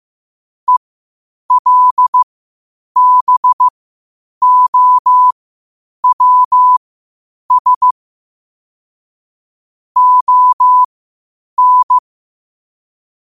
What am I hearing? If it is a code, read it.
Morse code, heard as a keyed tone, about 15 wpm, spaced as for 8 wpm, 1000 Hz. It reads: ELBOWS ON
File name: qotdfiles / 8wpm